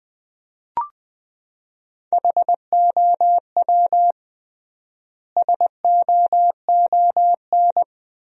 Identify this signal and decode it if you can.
Morse code, heard as a keyed tone, about 20 wpm, 700 Hz. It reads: HOW SOON
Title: 3x Word-Spacing - Sentences from Top 400 Words - 20wpm